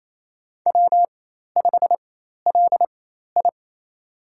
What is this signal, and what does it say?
Morse code, heard as a keyed tone, about 28 wpm, spaced as for 13 wpm, 700 Hz. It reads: W5LI
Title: USA Call Signs - 4 Characters - 4x Character Spacing - 28wpm